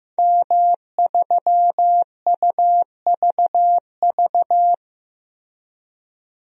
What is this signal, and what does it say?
Morse code, heard as a keyed tone, about 14 wpm, 700 Hz. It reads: M3UVV
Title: DXCC-100 Call Signs - 5 Characters - Repeated 3x - 15wpm